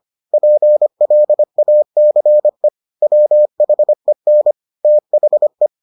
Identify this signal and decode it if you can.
Morse code, heard as a keyed tone, about 25 wpm, 600 Hz. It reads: PLACE WHEN THE